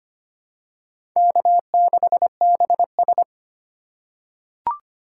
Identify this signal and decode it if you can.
Morse code, heard as a keyed tone, about 25 wpm, 700 Hz. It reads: K6BS
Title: USA Call Signs - 4 Characters - 2x Character Spacing - 25wpm